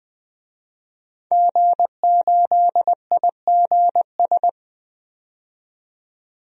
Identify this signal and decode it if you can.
Morse code, heard as a keyed tone, about 20 wpm, 700 Hz. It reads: G8IGS
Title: DXCC-100 Call Signs - 5 Characters - 20wpm